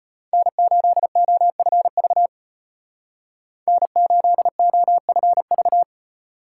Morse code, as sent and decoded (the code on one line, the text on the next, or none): -. ---.. --- ..-. ...- / -. ---.. --- ..-. ...-
N8OFV N8OFV